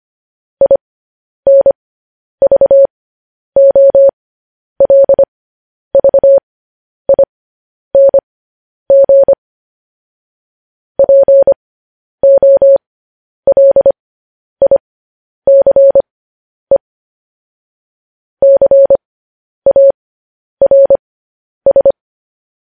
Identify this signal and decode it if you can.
Morse code, heard as a keyed tone, about 25 wpm, spaced as for 10 wpm, 550 Hz. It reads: INVOLVING POLICE CARS
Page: news headlines in morse code at 10 wpm